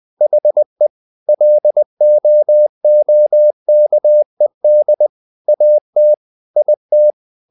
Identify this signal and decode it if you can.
Morse code, heard as a keyed tone, about 20 wpm, 600 Hz. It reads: HE LOOKED AT IT
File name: the old man and the sea / the old man and the sea part 1 20wpm